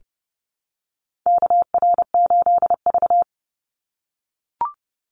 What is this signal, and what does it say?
Morse code, heard as a keyed tone, about 30 wpm, 700 Hz. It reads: KR8V
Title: Call Signs - 30wpm